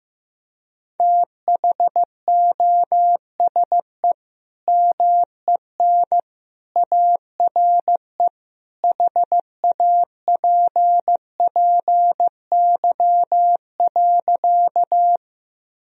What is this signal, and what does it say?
Morse code, heard as a keyed tone, about 16 wpm, 700 Hz. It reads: THOSE MEN ARE HAPPY.
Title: Sentences from Top 1000 Words - Review Part 1 - 15wpm